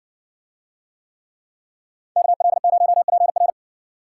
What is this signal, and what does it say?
Morse code, heard as a keyed tone, about 60 wpm, 700 Hz. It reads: KL0PR